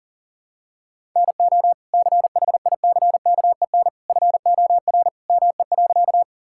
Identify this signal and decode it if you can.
Morse code, heard as a keyed tone, about 37 wpm, 700 Hz. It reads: NO CHICKEN FOR ME.
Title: Sentences from Top 1000 Words - Review Part 5 - 40wpm